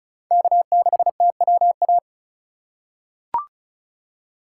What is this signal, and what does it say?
Morse code, heard as a keyed tone, about 34 wpm, 700 Hz. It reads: K6TWA
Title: USA Call Signs - 2x Character Spacing - 35wpm